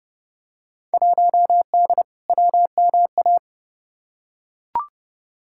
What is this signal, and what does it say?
Morse code, heard as a keyed tone, about 30 wpm, 700 Hz. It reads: 1D WMA
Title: Field Day Exchanges - 30wpm